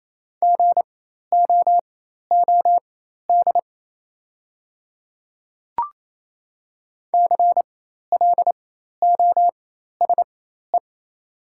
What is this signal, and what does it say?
Morse code, heard as a keyed tone, about 28 wpm, spaced as for 13 wpm, 700 Hz. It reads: GOOD CLOSE
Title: 4x Character Spacing - Top 500 Words - 28wpm